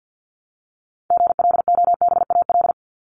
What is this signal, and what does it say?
Morse code, heard as a keyed tone, about 50 wpm, 700 Hz. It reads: GLOBAL
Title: Top 1001-1250 Words 50wpm